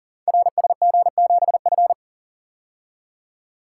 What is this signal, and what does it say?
Morse code, heard as a keyed tone, about 38 wpm, 700 Hz. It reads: RSG7F
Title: Code Groups - 5 Letter:Numbers - 40wpm